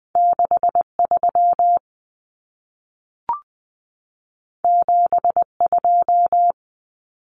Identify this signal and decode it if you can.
Morse code, heard as a keyed tone, about 20 wpm, 700 Hz. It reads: 63 72
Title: Common Cut Numbers - 2-Digit 20wpm